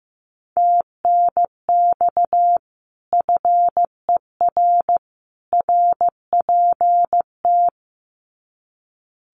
Morse code, heard as a keyed tone, about 15 wpm, 700 Hz. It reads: TNX FER RPT
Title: Common QSO Phrases 15wpm